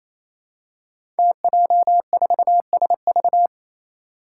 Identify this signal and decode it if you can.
Morse code, heard as a keyed tone, about 28 wpm, 700 Hz. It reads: TJ4SV
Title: Five Challenging Letter-Number Combinations 28wpm